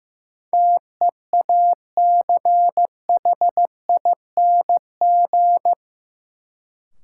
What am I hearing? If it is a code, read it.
Morse code, heard as a keyed tone, about 15 wpm, 700 Hz. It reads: TEACHING
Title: Sentences from Top 1100 Words - New Word Drill - 15wpm